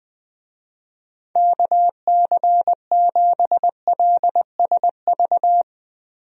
Morse code, heard as a keyed tone, about 20 wpm, 700 Hz. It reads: KC7LSV
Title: USA Call Signs - 6 Characters - 20wpm